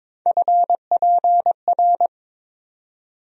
Morse code, heard as a keyed tone, about 22 wpm, 700 Hz. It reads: FPR
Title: USA Call Sign Postfixes - 3 Letters 22wpm